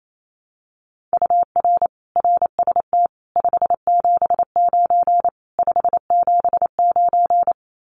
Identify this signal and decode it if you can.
Morse code, heard as a keyed tone, about 28 wpm, 700 Hz. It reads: UR RST 579 579
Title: Common QSO Phrases Encore 28wpm